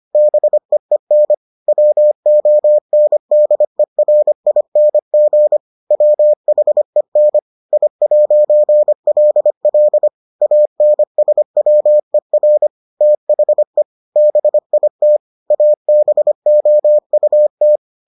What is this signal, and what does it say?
Morse code, heard as a keyed tone, about 25 wpm, 600 Hz. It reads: BEEN WONDERING WHEN I'LL ANSWER THE BIT ABOUT